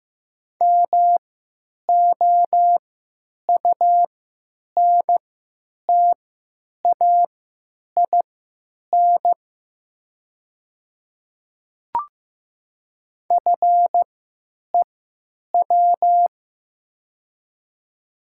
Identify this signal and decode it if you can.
Morse code, heard as a keyed tone, about 15 wpm, spaced as for 9 wpm, 700 Hz. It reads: MOUNTAIN FEW
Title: 3x Character Spacing - Top 300 Words - 15wpm